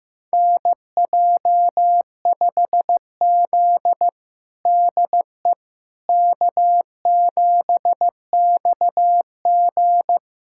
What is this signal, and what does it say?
Morse code, heard as a keyed tone, about 15 wpm, 700 Hz. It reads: NJ5Z DE K7XG
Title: USA Callsign DE Callsign - 4 Characters - 15wpm